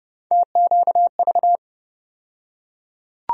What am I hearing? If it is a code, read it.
Morse code, heard as a keyed tone, about 31 wpm, 700 Hz. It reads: TQV
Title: USA Call Sign Postfixes - 3 Letters 30wpm